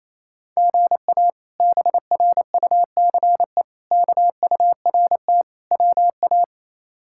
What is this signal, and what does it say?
Morse code, heard as a keyed tone, about 28 wpm, 700 Hz. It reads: GA BRUCE KURT WA
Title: K1USN SST QSO Practice - 28wpm